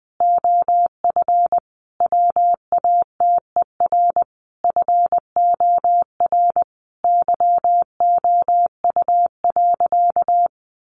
Morse code, heard as a keyed tone, about 20 wpm, 700 Hz. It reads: OF WATER FOR YOU.
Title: Sentences from Top 1300 Words 20wpm